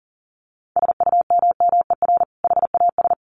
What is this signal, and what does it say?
Morse code, heard as a keyed tone, about 40 wpm, 700 Hz. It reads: SUMMER HAS
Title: Sentences from Top 1100 Words 40wpm